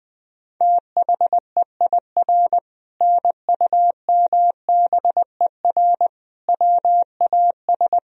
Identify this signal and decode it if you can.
Morse code, heard as a keyed tone, about 20 wpm, 700 Hz. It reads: THEIR NUMBER WAS
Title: Sentences from Top 1000 Words - Review Part 1 - 20wpm